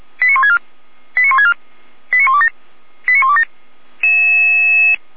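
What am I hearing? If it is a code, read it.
Five-tone selective call (ZVEI), heard as quick runs of five tones, tones of about 70 ms. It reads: 89164 89164 89127 89127
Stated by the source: ZVEI five-tone selective call received by SDR.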